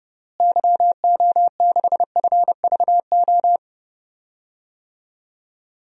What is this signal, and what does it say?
Morse code, heard as a keyed tone, about 30 wpm, 700 Hz. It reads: YO6FVO
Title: DXCC-100 Call Signs - 6 Characters - Repeated 3x - 30wpm